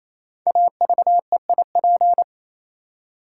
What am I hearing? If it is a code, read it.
Morse code, heard as a keyed tone, about 27 wpm, 700 Hz. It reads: AVEIP
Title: Code Groups - 5 Letters - 28wpm